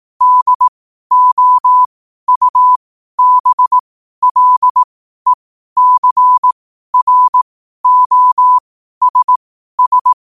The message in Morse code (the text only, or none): DOUBLECROSS